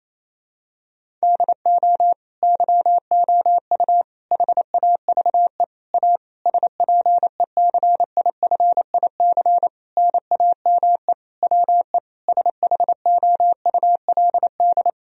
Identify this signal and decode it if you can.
Morse code, heard as a keyed tone, about 28 wpm, 700 Hz. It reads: DO YOU HAVE A SPECIFIC NAME WE SHOULD
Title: Sentences from Top 1000 Words - Review Part 3 - 28wpm